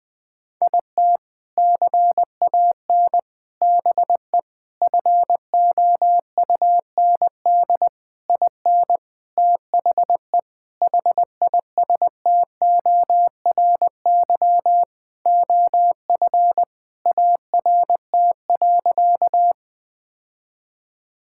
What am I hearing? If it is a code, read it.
Morse code, heard as a keyed tone, about 20 wpm, 700 Hz. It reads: IT CAN BE FOUND IN THE HISTORY OF ART.
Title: Sentences with 4-Word Phrases - Spoken Beginning - 20wpm